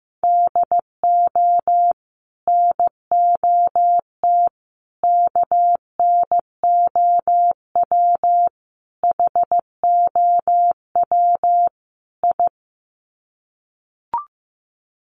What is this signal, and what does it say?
Morse code, heard as a keyed tone, about 15 wpm, 700 Hz. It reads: DO NOT KNOW HOW I
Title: Sets of 5 Words Encore2 15wpm